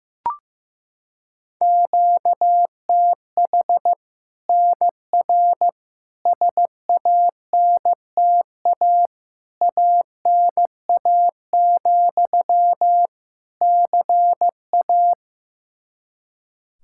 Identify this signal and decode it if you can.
Morse code, heard as a keyed tone, about 15 wpm, 700 Hz. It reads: QTH NR SANTA ANA, CA
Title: Practice QSO - Exchange 1 - Beginning - 15wpm